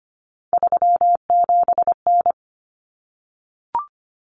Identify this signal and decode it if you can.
Morse code, heard as a keyed tone, about 25 wpm, 700 Hz. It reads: 37N
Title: Common Cut Numbers - 3-Digit 25wpm